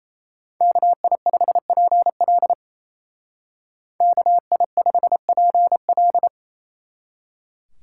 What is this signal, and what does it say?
Morse code, heard as a keyed tone, about 33 wpm, 700 Hz. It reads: KI5PL KI5PL
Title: Speed Racing - USA 2x2 Call Signs 28wpm